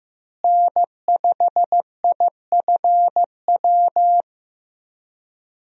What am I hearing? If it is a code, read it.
Morse code, heard as a keyed tone, about 15 wpm, 700 Hz. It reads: N5IFW
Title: USA Call Signs - 5 Characters - Repeated 2x - 15wpm